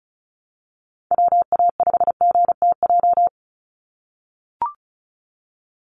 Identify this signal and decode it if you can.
Morse code, heard as a keyed tone, about 34 wpm, 700 Hz. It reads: WA5GTJ
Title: USA Call Signs 35wpm